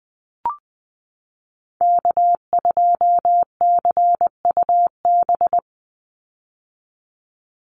Morse code, heard as a keyed tone, about 20 wpm, 700 Hz. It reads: K2CUB
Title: Call Signs - 20wpm